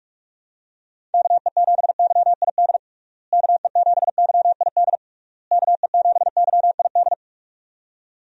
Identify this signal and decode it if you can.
Morse code, heard as a keyed tone, about 46 wpm, 700 Hz. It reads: KE7YID KE7YID KE7YID
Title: USA Call Signs - 6 Characters - Repeated 3x - 45wpm